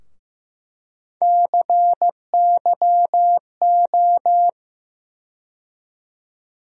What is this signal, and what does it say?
Morse code, heard as a keyed tone, about 15 wpm, 700 Hz. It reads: CYO
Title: USA Call Sign Postfixes - 3 Letters 15wpm